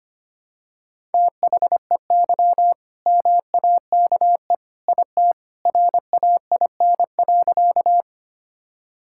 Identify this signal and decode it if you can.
Morse code, heard as a keyed tone, about 25 wpm, 700 Hz. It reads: THEY MAKE IT RAIN.
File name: practice / 25wpm / All-Age Appropriate Jokes - 25wpm